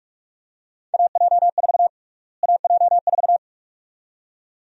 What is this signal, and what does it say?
Morse code, heard as a keyed tone, about 46 wpm, 690 Hz. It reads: AJ4 AJ4
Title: USA Call Sign Prefixes - Repeated 2x - 45wpm